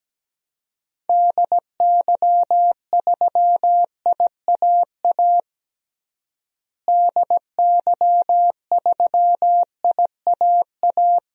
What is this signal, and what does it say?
Morse code, heard as a keyed tone, about 17 wpm, 700 Hz. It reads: DY3IAA DY3IAA
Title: DXCC-100 Call Signs - 6 Characters - Repeated 3x - 17wpm